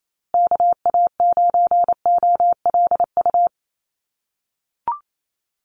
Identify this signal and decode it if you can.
Morse code, heard as a keyed tone, about 28 wpm, 700 Hz. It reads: KA9OLU
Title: USA Call Signs Repeated 2x 28wpm